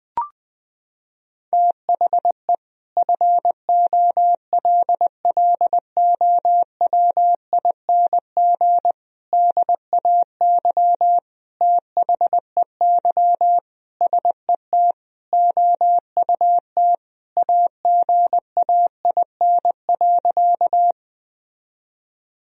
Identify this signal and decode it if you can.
Morse code, heard as a keyed tone, about 20 wpm, 700 Hz. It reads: THE FOLLOWING DAY THEY SET OUT AGAIN.
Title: Sentences from Top 1000 Words - Review Part 4 - 20wpm